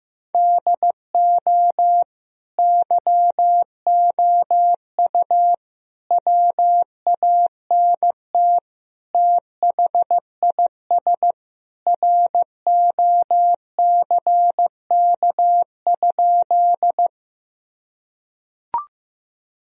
Morse code, heard as a keyed tone, about 15 wpm, 700 Hz. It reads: DO YOU WANT THIS ROCK?
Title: Sentences from Top 1000 Words - Review Part 2 - 15wpm